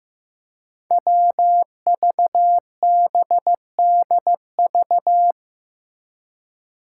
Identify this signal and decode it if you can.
Morse code, heard as a keyed tone, about 15 wpm, 700 Hz. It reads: WVBDV
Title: Five Challenging Letter-Number Combinations 15wpm